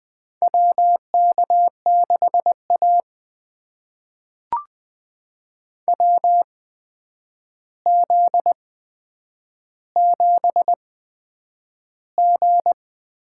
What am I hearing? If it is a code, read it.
Morse code, heard as a keyed tone, about 20 wpm, spaced as for 9 wpm, 700 Hz. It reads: WK6A WZ7G
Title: USA Call Signs - 4 Characters - 8x Character Spacing - 20wpm